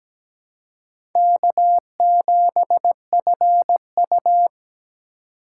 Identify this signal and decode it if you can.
Morse code, heard as a keyed tone, about 17 wpm, 700 Hz. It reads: K7FU
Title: USA Call Signs - 4 Characters - 3x Character Spacing - 17wpm